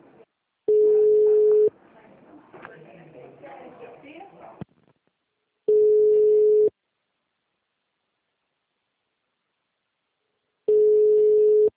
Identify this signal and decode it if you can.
Ringback tone (Europe, 425 Hz), with speech.
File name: out- -2016- - - .24641